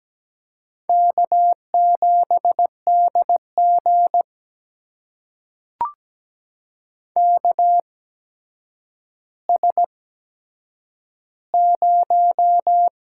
Morse code, heard as a keyed tone, about 17 wpm, spaced as for 8 wpm, 700 Hz. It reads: K7DG KS0
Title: USA Call Signs - 4 Characters - 8x Character Spacing - 17wpm